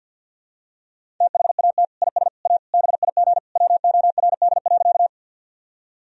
Morse code, heard as a keyed tone, about 47 wpm, 700 Hz. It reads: THAT IS A BIG WORD.
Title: Sentences from Top 1000 Words - Review Part 1 - 50wpm